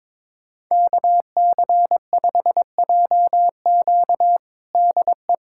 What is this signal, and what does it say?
Morse code, heard as a keyed tone, about 22 wpm, 700 Hz. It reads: KC5JQ DE
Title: USA Callsign DE Callsign - 5 Characters - 22wpm